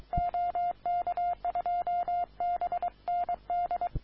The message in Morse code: OK2BND

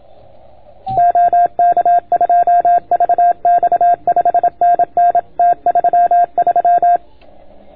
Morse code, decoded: OK2VX 5NN T33

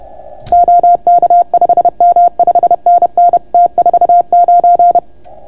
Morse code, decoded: OK5M 5NN T49

